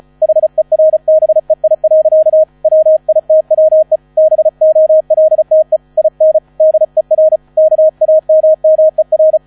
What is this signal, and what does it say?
Morse code, heard as a keyed tone, about 34 wpm, 630 Hz, with radio noise. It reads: HERBEI. WITWE BOLTE IN DER KAMMER
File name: 35wpm